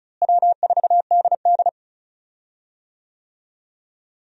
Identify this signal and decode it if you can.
Morse code, heard as a keyed tone, about 35 wpm, 700 Hz. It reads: W4DD
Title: USA Call Signs - 4 Characters - Repeated 3x - 35wpm